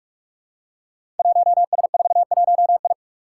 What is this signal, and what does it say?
Morse code, heard as a keyed tone, about 45 wpm, 700 Hz. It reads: 1S41I
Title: Five Challenging Letter-Number Combinations 45wpm